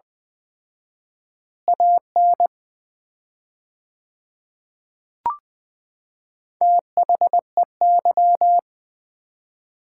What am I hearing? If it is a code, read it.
Morse code, heard as a keyed tone, about 21 wpm, 700 Hz. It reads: AN THEY
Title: 4x Word-Spacing - Sets of 2 Words - 20wpm